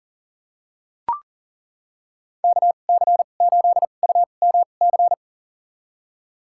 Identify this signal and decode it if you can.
Morse code, heard as a keyed tone, about 40 wpm, spaced as for 29 wpm, 700 Hz. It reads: KC8UMC
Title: USA Call Signs - 6 Characters - 2x Character Spacing - 40wpm